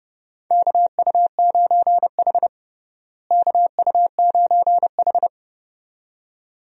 Morse code, heard as a keyed tone, about 30 wpm, 700 Hz. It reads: KU9H KU9H
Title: USA Call Signs - 4 Characters - Repeated 2x - 30wpm